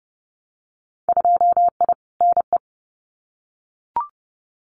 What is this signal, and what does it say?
Morse code, heard as a keyed tone, about 31 wpm, 700 Hz. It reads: 2I NE
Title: Winter Field Day Exchanges Encore 30wpm